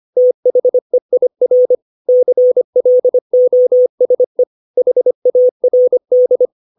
Morse code, heard as a keyed tone, about 25 wpm, 500 Hz. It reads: THEIR CLOSE HARD
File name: 632 words / 632 Words 25WPM